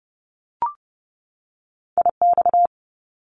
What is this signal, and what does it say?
Morse code, heard as a keyed tone, about 30 wpm, 700 Hz. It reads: IX
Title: Top Two Letter Combinations in Words 30wpm